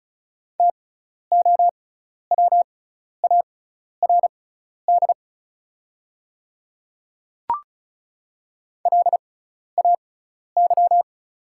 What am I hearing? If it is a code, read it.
Morse code, heard as a keyed tone, about 36 wpm, spaced as for 12 wpm, 700 Hz. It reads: TOWARD LAY
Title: 6x Character Spacing - Top 500 Words - 35wpm